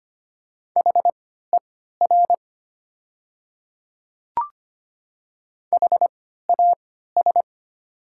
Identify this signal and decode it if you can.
Morse code, heard as a keyed tone, about 25 wpm, spaced as for 14 wpm, 700 Hz. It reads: HER HAS
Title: 3x Character Spacing - Top 100 Words - 25wpm